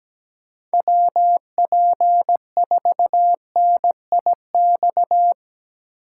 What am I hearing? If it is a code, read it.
Morse code, heard as a keyed tone, about 18 wpm, 700 Hz. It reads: WP4NIX
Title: Speed Racing - USA 2x3 Call Signs 17wpm